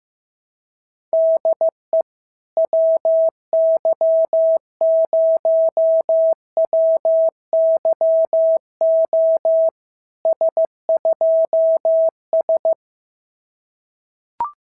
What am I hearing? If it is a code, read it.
Morse code, heard as a keyed tone, about 15 wpm, 650 Hz. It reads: DE WY0WYO S2S
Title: SOTA QSO Practice 15wpm